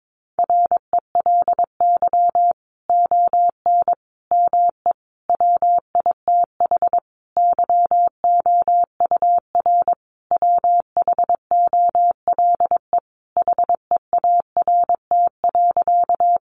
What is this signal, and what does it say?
Morse code, heard as a keyed tone, about 22 wpm, 700 Hz. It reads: RELY ON ME WITH YOUR WHOLE HEART.
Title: Sentences from Top 1200 Words 22wpm